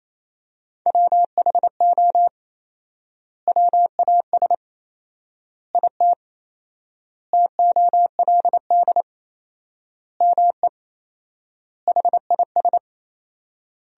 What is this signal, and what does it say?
Morse code, heard as a keyed tone, about 28 wpm, 700 Hz. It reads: WHO WAS IT TOLD ME HIS
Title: 4x Word-Spacing - Sentences from Top 400 Words - 28wpm